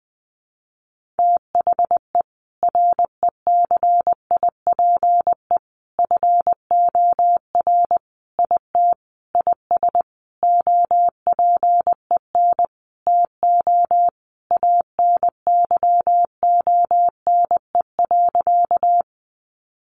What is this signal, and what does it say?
Morse code, heard as a keyed tone, about 20 wpm, 700 Hz. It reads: THE RECIPE FOR IT IS OPEN TO ANYONE.
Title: Sentences from Top 1200 Words 20wpm